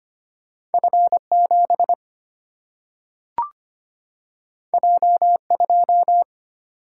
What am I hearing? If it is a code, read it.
Morse code, heard as a keyed tone, about 25 wpm, 700 Hz. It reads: F7 J2
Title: Single-Letter Single-Number 25wpm